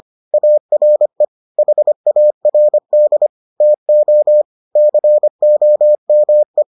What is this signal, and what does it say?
Morse code, heard as a keyed tone, about 25 wpm, 600 Hz. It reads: ARE HARD TO COME